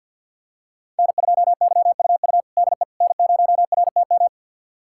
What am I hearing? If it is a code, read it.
Morse code, heard as a keyed tone, about 46 wpm, 700 Hz. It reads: N2YUU DE N0RTM
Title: USA Callsign DE Callsign - 5 Characters - 50wpm